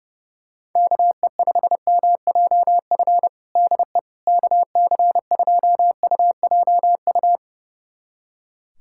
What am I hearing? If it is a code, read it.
Morse code, heard as a keyed tone, about 30 wpm, 700 Hz. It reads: KE5MJF DE KC2UJU
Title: USA Callsign DE Callsign - 6 Characters - 30wpm